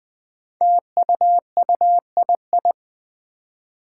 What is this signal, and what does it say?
Morse code, heard as a keyed tone, about 20 wpm, 700 Hz. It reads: TUUII
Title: Five Challenging Letter-Number Combinations 20wpm